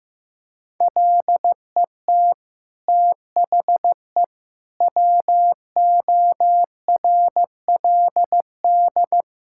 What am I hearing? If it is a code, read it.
Morse code, heard as a keyed tone, about 15 wpm, 700 Hz. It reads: LET THE WORLD
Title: Sentences from Top 1600 Words 15wpm